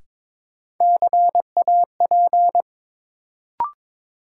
Morse code, heard as a keyed tone, about 22 wpm, 700 Hz. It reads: CAP
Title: Sentences from Top 1600 Words - New Word Drill - 22wpm